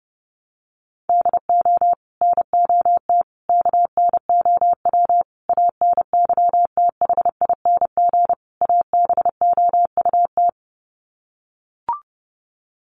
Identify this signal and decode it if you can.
Morse code, heard as a keyed tone, about 30 wpm, 700 Hz. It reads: DO NOT KNOW ANYTHING ABOUT
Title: Sets of 5 Words 30wpm